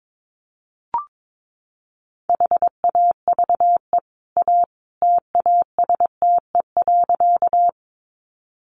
Morse code, heard as a keyed tone, about 22 wpm, 700 Hz. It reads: HAVE A TASTE.
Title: Sentences from Top 1100 Words 22wpm